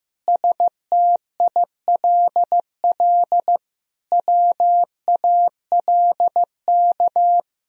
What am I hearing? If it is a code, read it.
Morse code, heard as a keyed tone, about 16 wpm, 700 Hz. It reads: STILL WALK
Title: Sentences from Top 1000 Words - Review Part 1 - 15wpm